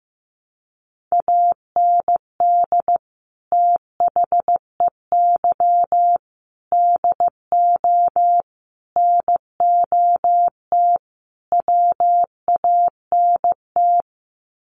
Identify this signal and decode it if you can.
Morse code, heard as a keyed tone, about 15 wpm, 700 Hz. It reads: AND THEY DO NOT WANT
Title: Sets of 5 Words Encore2 15wpm